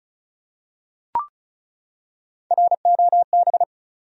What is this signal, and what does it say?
Morse code, heard as a keyed tone, about 34 wpm, 700 Hz. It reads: ROB
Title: USA Common Names - Max 4 Letters - 35wpm